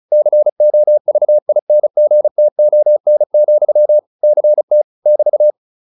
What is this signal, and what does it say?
Morse code, heard as a keyed tone, about 35 wpm, 600 Hz. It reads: COVINGTON, CT <BT>
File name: QSO403 35